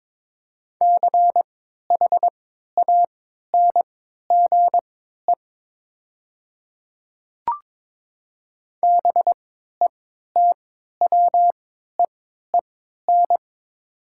Morse code, heard as a keyed tone, about 22 wpm, spaced as for 12 wpm, 700 Hz. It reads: CHANGE BETWEEN
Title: 3x Character Spacing - Top 300 Words - 22wpm